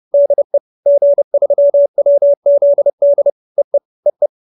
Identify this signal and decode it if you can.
Morse code, heard as a keyed tone, about 30 wpm, 570 Hz. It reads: DE G3WZD EE EE
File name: QSO310 30